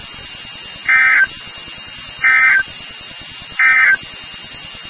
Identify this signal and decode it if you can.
Emergency Alert System (SAME) tones, heard as harsh buzzing bursts. It is heard from 0.8 to 4.0 s.
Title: NOAA Weather Radio WXK36